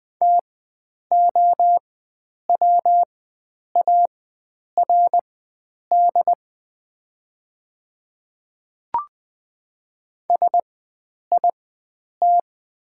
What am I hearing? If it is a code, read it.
Morse code, heard as a keyed tone, about 20 wpm, spaced as for 9 wpm, 700 Hz. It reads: TOWARD SIT
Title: 4x Character Spacing - Top 500 Words - 20wpm